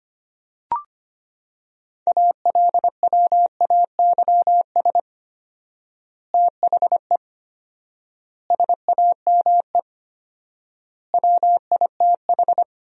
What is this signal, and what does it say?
Morse code, heard as a keyed tone, about 25 wpm, 700 Hz. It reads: ALWAYS THE SAME WITH
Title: 4x Word-Spacing - Sentences from Top 300 Words - 25wpm